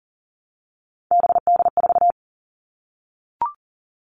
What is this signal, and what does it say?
Morse code, heard as a keyed tone, about 38 wpm, 700 Hz. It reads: BD4
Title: Callsign Prefixes for Top 100 DXCC - 40wpm